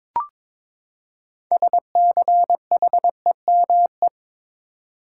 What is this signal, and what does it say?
Morse code, heard as a keyed tone, about 22 wpm, 700 Hz. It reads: SCHEME
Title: Six Letter Words - 22wpm